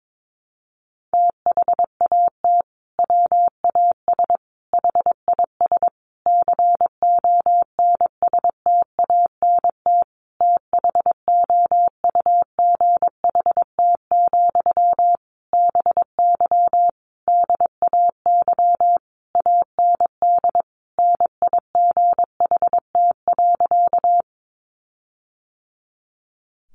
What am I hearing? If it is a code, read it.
Morse code, heard as a keyed tone, about 22 wpm, 700 Hz. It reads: THAT WAS HIS CONSTANT THOUGHT, BY DAY AND NIGHT.
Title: Sentences from Top 1500 Words 22wpm